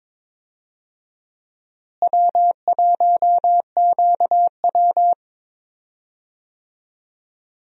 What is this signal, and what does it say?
Morse code, heard as a keyed tone, about 22 wpm, 700 Hz. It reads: W1QW
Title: USA Call Signs - 4 Characters - Repeated 2x - 22wpm